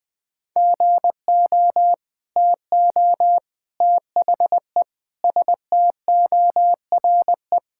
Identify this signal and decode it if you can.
Morse code, heard as a keyed tone, about 21 wpm, 700 Hz. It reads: GO TO THE STORE
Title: Sentences from Top 1000 Words - Review Part 5 - 20wpm